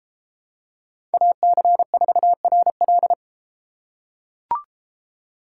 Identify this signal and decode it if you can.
Morse code, heard as a keyed tone, about 31 wpm, 700 Hz. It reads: AC4RL